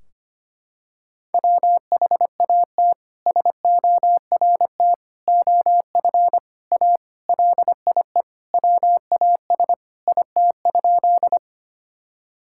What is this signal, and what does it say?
Morse code, heard as a keyed tone, about 25 wpm, 700 Hz. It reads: WHAT SORT OF A LIE WAS IT?
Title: Sentences from Top 1600 Words 25wpm